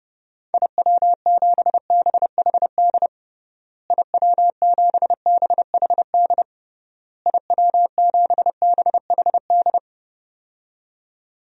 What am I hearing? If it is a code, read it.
Morse code, heard as a keyed tone, about 30 wpm, 700 Hz. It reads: IW7BHD IW7BHD IW7BHD
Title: DXCC-100 Call Signs - 6 Characters - Repeated 3x - 30wpm